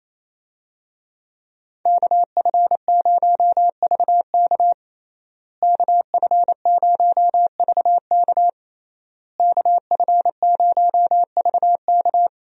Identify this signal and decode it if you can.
Morse code, heard as a keyed tone, about 28 wpm, 700 Hz. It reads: KF0VK KF0VK KF0VK
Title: USA Call Signs - 5 Characters - Repeated 3x - 28wpm